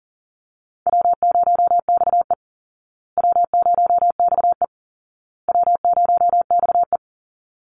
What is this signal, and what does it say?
Morse code, heard as a keyed tone, about 40 wpm, 700 Hz. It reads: W0XE W0XE W0XE
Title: Easy Call Signs Repeated 3x - 40wpm